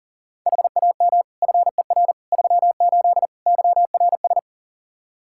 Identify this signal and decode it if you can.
Morse code, heard as a keyed tone, about 37 wpm, 700 Hz. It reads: HAM FER 38 YRS
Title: K1USN SST QSO Practice 40wpm